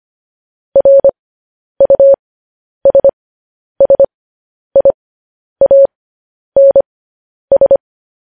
Morse code, heard as a keyed tone, about 25 wpm, spaced as for 10 wpm, 550 Hz. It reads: RUSSIANS